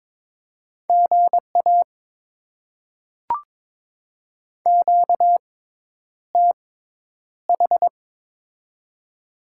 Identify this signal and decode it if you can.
Morse code, heard as a keyed tone, about 22 wpm, spaced as for 10 wpm, 700 Hz. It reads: GA QTH
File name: Top 20 QSO Elements - 6x Character Spacing - 22wpm